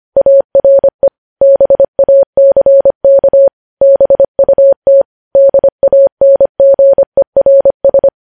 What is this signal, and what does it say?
Morse code, heard as a keyed tone, about 25 wpm, 550 Hz. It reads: ARE BACK BUT DANGERS
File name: podcast / News headlines-1x-25x25